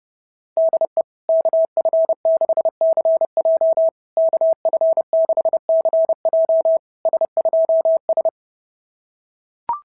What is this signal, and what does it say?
Morse code, heard as a keyed tone, about 26 wpm, 660 Hz. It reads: DE KF6CJ KF6CJ S2S
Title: SOTA QSO Practice 30wpm